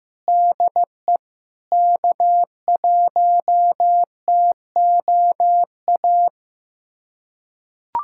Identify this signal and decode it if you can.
Morse code, heard as a keyed tone, about 15 wpm, 700 Hz. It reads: DE K1TOA
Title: USA Callsign DE Callsign - 5 Characters - 15wpm